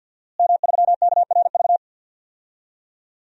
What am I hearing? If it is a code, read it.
Morse code, heard as a keyed tone, about 51 wpm, 700 Hz. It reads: M3KRV
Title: DXCC-100 Call Signs - 5 Characters - Repeated 2x - 50wpm